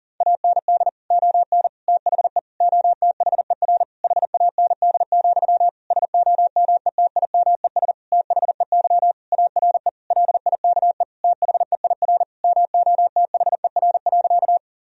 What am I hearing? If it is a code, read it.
Morse code, heard as a keyed tone, about 37 wpm, 700 Hz. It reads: AND ON THE OTHER HAND, SOMETIMES THEY ARE LIKE THEIR MOTHER.
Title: Sentences from Top 1000 Words - Review Part 3 - 40wpm